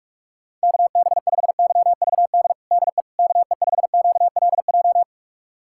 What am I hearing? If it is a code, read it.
Morse code, heard as a keyed tone, about 46 wpm, 700 Hz. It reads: KB5YVD DE KE5QLJ
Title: USA Callsign DE Callsign - 6 Characters - 45wpm